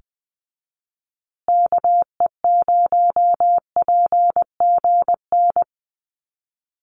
Morse code, heard as a keyed tone, about 20 wpm, 700 Hz. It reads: KE0PGN
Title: USA Call Signs Repeated 3x 20wpm